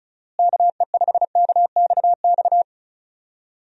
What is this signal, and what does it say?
Morse code, heard as a keyed tone, about 36 wpm, 700 Hz. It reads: KE5KXX
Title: USA Call Signs - 6 Characters - Repeated 2x - 35wpm